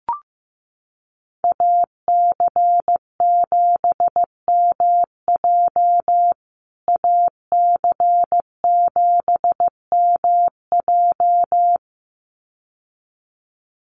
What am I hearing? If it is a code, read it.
Morse code, heard as a keyed tone, about 15 wpm, 700 Hz. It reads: AC7MJ AC7MJ
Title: USA Call Signs Repeated 2x 15wpm